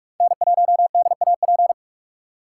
Morse code, heard as a keyed tone, about 43 wpm, 700 Hz. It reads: N1DAP
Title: USA Call Signs - 5 Characters - 3x Character Spacing - 45wpm